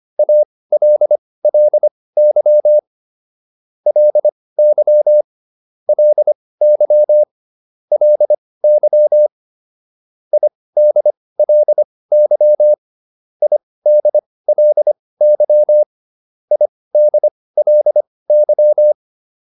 Morse code, heard as a keyed tone, about 25 wpm, spaced as for 18 wpm, 600 Hz. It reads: ALLY LY LY LY IDLY IDLY IDLY